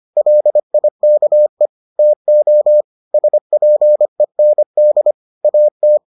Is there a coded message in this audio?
Yes: Morse code from 0.2 to 6.0 s, heard as a keyed tone, about 25 wpm, 600 Hz. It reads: LIKE TO SPEND AT